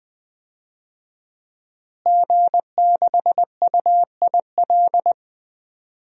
Morse code, heard as a keyed tone, about 20 wpm, 700 Hz. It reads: G6UIL
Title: DXCC-100 Call Signs - 5 Characters - Repeated 3x - 20wpm